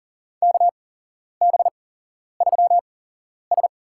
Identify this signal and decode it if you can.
Morse code, heard as a keyed tone, about 40 wpm, spaced as for 11 wpm, 700 Hz. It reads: KB3S
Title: USA Call Signs - 4 Characters - 8x Character Spacing - 40wpm